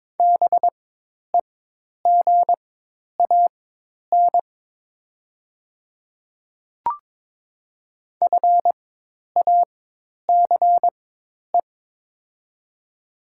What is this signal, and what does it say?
Morse code, heard as a keyed tone, about 22 wpm, spaced as for 10 wpm, 690 Hz. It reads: BEGAN FACE
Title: 4x Character Spacing - Top 300 Words - 22wpm